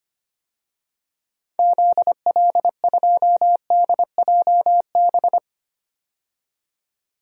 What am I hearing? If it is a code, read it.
Morse code, heard as a keyed tone, about 25 wpm, 700 Hz. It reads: ZL2DJB
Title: DXCC-100 Call Signs - 6 Characters - 25wpm